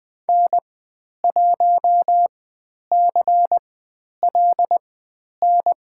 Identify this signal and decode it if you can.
Morse code, heard as a keyed tone, about 20 wpm, spaced as for 10 wpm, 700 Hz. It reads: N1CLN
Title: 095 Lesson 16g - Letter c - Call Signs 20wpm